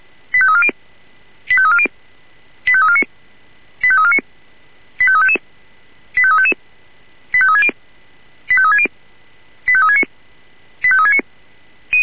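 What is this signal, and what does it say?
Five-tone selective call (ZVEI), heard as quick runs of five tones, tones of about 70 ms. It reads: 85340 85340 85349 85349 85355 85355 85360 85360 85369 85369
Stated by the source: ZVEI five-tone selective call received by SDR.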